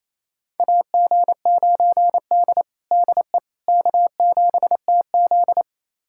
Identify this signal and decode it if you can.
Morse code, heard as a keyed tone, about 28 wpm, 700 Hz. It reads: AG9D DE K7TZ
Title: USA Callsign DE Callsign - 4 Characters - 28wpm